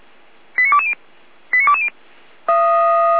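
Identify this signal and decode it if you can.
Five-tone selective call (ZVEI), heard as quick runs of five tones, tones of about 70 ms. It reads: 89220 89220
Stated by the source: ZVEI five-tone call 89223 received via SDR, from Feuerwehr Orlamünde.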